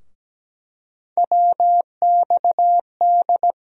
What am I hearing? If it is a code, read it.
Morse code, heard as a keyed tone, about 17 wpm, 700 Hz. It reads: WXD
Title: USA Call Sign Postfixes - 3 Letters 17wpm